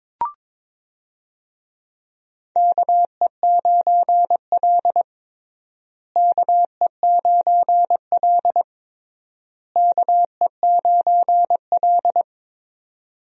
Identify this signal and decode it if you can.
Morse code, heard as a keyed tone, about 22 wpm, 700 Hz. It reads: KE9L KE9L KE9L
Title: USA Call Signs - 4 Characters - Repeated 3x - 22wpm